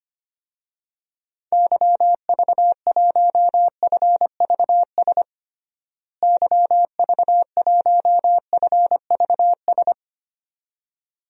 Morse code, heard as a keyed tone, about 25 wpm, 700 Hz. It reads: YV1FVS YV1FVS
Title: DXCC-100 Call Signs - 6 Characters - Repeated 2x - 25wpm